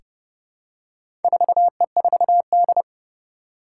Morse code, heard as a keyed tone, about 31 wpm, 700 Hz. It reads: 4E4D
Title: Four Challenging Letter-Number Combinations 30wpm